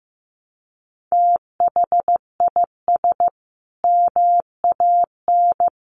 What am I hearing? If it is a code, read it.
Morse code, heard as a keyed tone, about 15 wpm, 700 Hz. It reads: THIS MAN
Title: Sentences from Top 1200 Words 15wpm